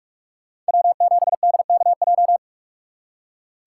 Morse code, heard as a keyed tone, about 59 wpm, 700 Hz. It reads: W7DKJ